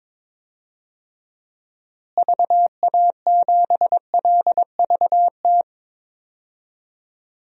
Morse code, heard as a keyed tone, about 22 wpm, 700 Hz. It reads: VA7LVT
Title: DXCC-100 Call Signs - 6 Characters - Repeated 3x - 22wpm